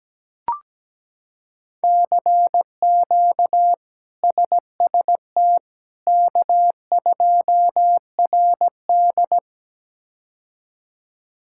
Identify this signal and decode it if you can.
Morse code, heard as a keyed tone, about 17 wpm, 700 Hz. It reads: CQ SST K2RD
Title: K1USN SST QSO Practice - 17wpm